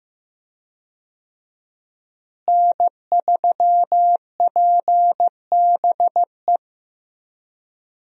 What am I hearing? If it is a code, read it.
Morse code, heard as a keyed tone, about 15 wpm, 700 Hz. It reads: N3PBE
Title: USA Call Signs - 5 Characters - Repeated 3x - 15wpm